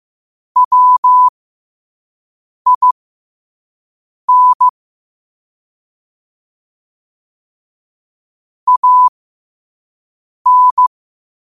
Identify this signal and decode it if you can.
Morse code, heard as a keyed tone, about 15 wpm, spaced as for 5 wpm, 1000 Hz. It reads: WIN AN
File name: 5wpm